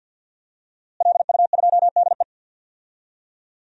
Morse code, heard as a keyed tone, about 53 wpm, 700 Hz. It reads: LU2DE
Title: DXCC-100 Call Signs - 5 Characters - Repeated 3x - 50wpm